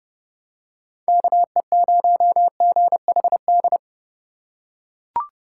Morse code, heard as a keyed tone, about 29 wpm, 700 Hz. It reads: KE0GHD